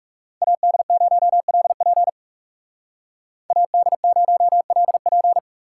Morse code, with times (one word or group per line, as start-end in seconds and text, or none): 0.4-2.1 AD0LP
3.5-5.4 AD0LP